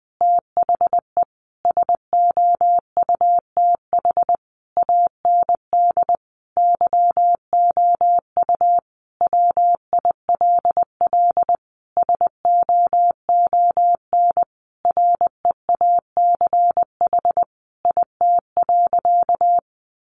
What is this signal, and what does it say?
Morse code, heard as a keyed tone, about 20 wpm, 700 Hz. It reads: THE SOUTH AND YOU WILL SOON REACH IT.
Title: Sentences from Top 400 Words 20wpm